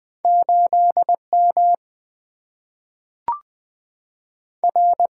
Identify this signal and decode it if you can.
Morse code, heard as a keyed tone, about 20 wpm, 700 Hz. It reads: 8M R
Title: Single-Letter Single-Number 20wpm